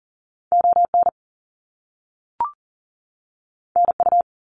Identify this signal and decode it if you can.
Morse code, heard as a keyed tone, about 40 wpm, 700 Hz. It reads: ON NU
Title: Canadian Provinces and Territories 40wpm